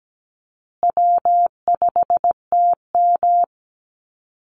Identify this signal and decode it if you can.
Morse code, heard as a keyed tone, about 17 wpm, 700 Hz. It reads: W5TM
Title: Call Signs Repeated 2x - 17wpm